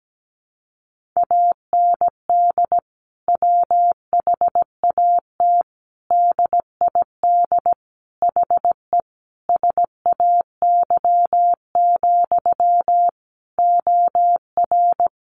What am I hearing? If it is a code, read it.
Morse code, heard as a keyed tone, about 17 wpm, 700 Hz. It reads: AND WHAT DID HE SAY, OR
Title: Sentences from Top 700 Words 17wpm